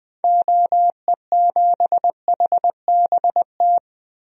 Morse code, heard as a keyed tone, about 20 wpm, 700 Hz. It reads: OE7HBT
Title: DXCC-100 Call Signs - 6 Characters - 20wpm